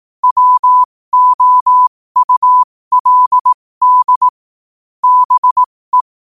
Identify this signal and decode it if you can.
Morse code, heard as a keyed tone, about 18 wpm, spaced as for 15 wpm, 1000 Hz. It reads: WOULD BE
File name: qotdfiles / 15wpm